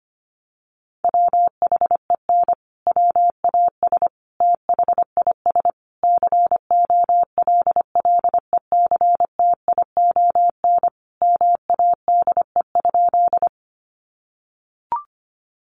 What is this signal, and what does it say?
Morse code, heard as a keyed tone, about 25 wpm, 700 Hz. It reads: WHEN WAS THIS COLLECTION MADE?
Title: Sentences from Top 1200 Words 25wpm